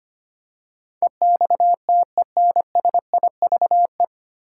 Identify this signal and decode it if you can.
Morse code, heard as a keyed tone, about 25 wpm, 700 Hz. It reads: EXTENSIVE
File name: Nine Letter Words - 25wpm